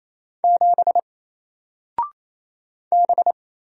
Morse code, heard as a keyed tone, about 27 wpm, 700 Hz. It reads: 7 B
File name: Single-Letter-Number-28wpm